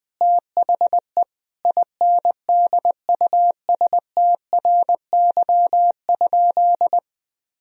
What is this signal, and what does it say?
Morse code, heard as a keyed tone, about 20 wpm, 700 Hz. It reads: THE INDUSTRY?
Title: Sentences from Top 1000 Words - Review Part 4 - 20wpm